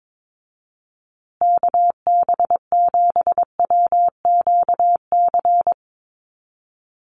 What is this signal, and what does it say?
Morse code, heard as a keyed tone, about 22 wpm, 700 Hz. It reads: KB7WQC
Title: USA Call Signs Repeated 2x 22wpm